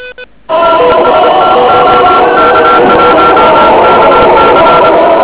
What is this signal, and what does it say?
Morse code, heard as a keyed tone, about 24 wpm, 1360 Hz, with radio noise. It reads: TEST OK1KC
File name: OK QRP 2007 / REC14